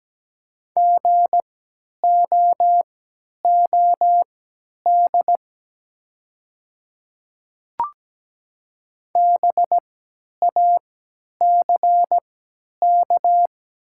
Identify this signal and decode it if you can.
Morse code, heard as a keyed tone, about 17 wpm, spaced as for 10 wpm, 700 Hz. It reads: GOOD BACK
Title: 3x Character Spacing - Top 200 Words - 17wpm